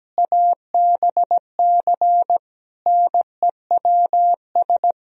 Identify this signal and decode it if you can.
Morse code, heard as a keyed tone, about 17 wpm, 700 Hz. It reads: ABC NEWS
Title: Speed-Racing - Sets of 5 Words 17wpm